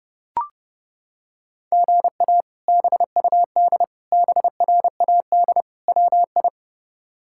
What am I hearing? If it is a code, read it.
Morse code, heard as a keyed tone, about 30 wpm, 700 Hz. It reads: GA BUD BRAD WI
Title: K1USN SST QSO Practice - 30wpm